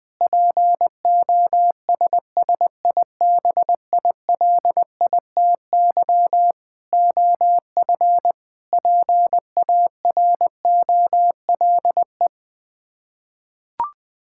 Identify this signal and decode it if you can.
Morse code, heard as a keyed tone, about 21 wpm, 700 Hz. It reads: POSSIBILITY OF PAROLE
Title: Sets of 8 Words - 20wpm